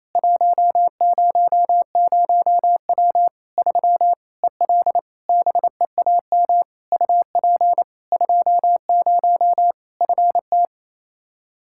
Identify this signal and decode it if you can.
Morse code, heard as a keyed tone, about 28 wpm, 700 Hz. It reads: 100W 3 EL BEAM UP 20 FT
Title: Practice QSO - Exchange 2 - Info - 28wpm